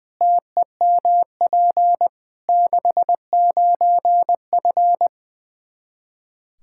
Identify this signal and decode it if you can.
Morse code, heard as a keyed tone, about 21 wpm, 700 Hz. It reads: TEMP 69F
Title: K1USN SST QSO Practice 20wpm